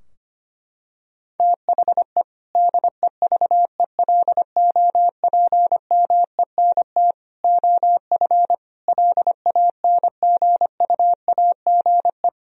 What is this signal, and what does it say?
Morse code, heard as a keyed tone, about 25 wpm, 700 Hz. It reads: THE DEVELOPMENT OF LANGUAGE
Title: Sentences from Top 1000 Words - Review Part 4 - 25wpm